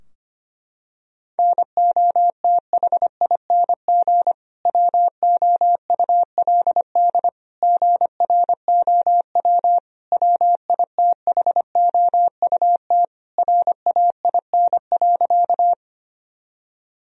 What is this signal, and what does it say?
Morse code, heard as a keyed tone, about 25 wpm, 700 Hz. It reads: NOTHING WOULD GROW WITHOUT RAIN.
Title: Sentences from Top 1000 Words - Review Part 3 - 25wpm